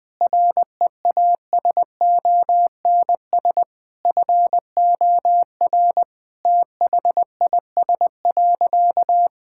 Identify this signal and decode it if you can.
Morse code, heard as a keyed tone, about 20 wpm, 700 Hz. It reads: REASONS FOR THIS.
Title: Sentences with 5-Word Phrases - 20wpm